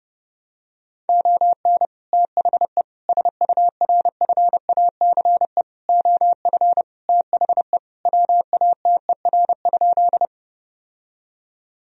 Morse code, heard as a keyed tone, about 30 wpm, 700 Hz. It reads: ON THE SURFACE OF THE WATER?
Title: Sentences from Top 1000 Words - Review Part 3 - 30wpm